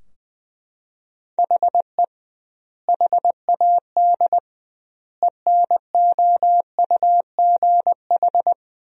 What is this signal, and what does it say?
Morse code, heard as a keyed tone, about 20 wpm, 700 Hz. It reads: HE HAD ENOUGH
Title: 2x Word-Spacing - Sentences from Top 400 Words - 20wpm